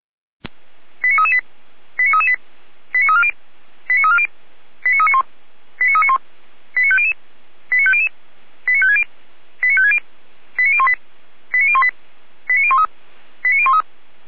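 Five-tone selective call (ZVEI), heard as quick runs of five tones, tones of about 70 ms. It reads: 89338 89338 89340 89340 89391 89391 89500 89500 89560 89560 89018 89018 89013 89013
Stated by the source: ZVEI five-tone selective call received on an SDR, fire brigade feuerwehr-orlamuende.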